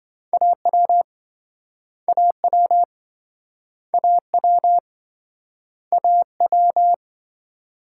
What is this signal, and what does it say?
Morse code, heard as a keyed tone, about 27 wpm, 700 Hz. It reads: AW AW AW AW